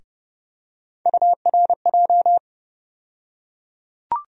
Morse code, heard as a keyed tone, about 30 wpm, 700 Hz. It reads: URJ
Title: USA Call Sign Postfixes - 3 Letters 30wpm